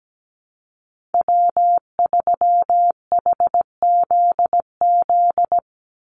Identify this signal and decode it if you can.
Morse code, heard as a keyed tone, about 17 wpm, 700 Hz. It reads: W3HZZ
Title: Call Signs Repeated 3x - 17wpm